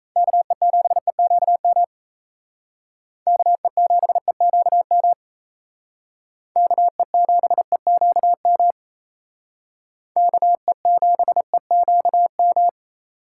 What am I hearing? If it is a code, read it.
Morse code, heard as a keyed tone, about 36 wpm, 700 Hz. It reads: KE7EQM KE7EQM KE7EQM KE7EQM